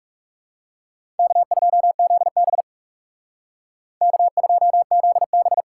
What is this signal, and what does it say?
Morse code, heard as a keyed tone, about 36 wpm, 700 Hz. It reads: K2ZB K2ZB
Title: Speed Racing - USA 1x2 Call Signs 30wpm